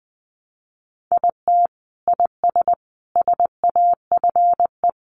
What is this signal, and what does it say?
Morse code, heard as a keyed tone, about 20 wpm, 700 Hz. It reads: IT IS SAFE
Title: Sentences from Top 900 Words 20wpm